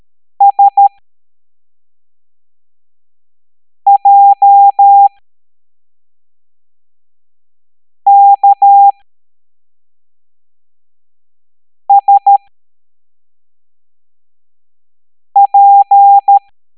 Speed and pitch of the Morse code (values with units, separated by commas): 13 wpm, spaced as for 3 wpm, 800 Hz